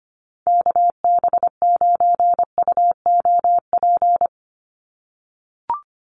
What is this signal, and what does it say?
Morse code, heard as a keyed tone, about 25 wpm, 700 Hz. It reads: KB9UOP
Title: USA Call Signs 25wpm